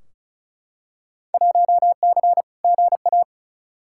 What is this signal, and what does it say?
Morse code, heard as a keyed tone, about 36 wpm, 700 Hz. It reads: 1C GA
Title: Field Day Exchanges Encore - 35wpm